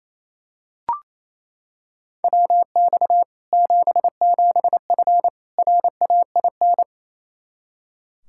Morse code, heard as a keyed tone, about 28 wpm, 700 Hz. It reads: WX 77F RAIN
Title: Practice QSO - Exchange 2 - Info - 28wpm